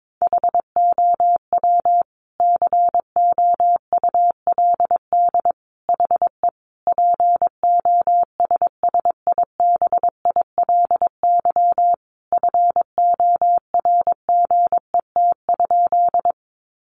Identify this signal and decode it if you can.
Morse code, heard as a keyed tone, about 22 wpm, 700 Hz. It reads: HOW COULD HE POSSIBILY FORGET?
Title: Sentences from Top 1300 Words 22wpm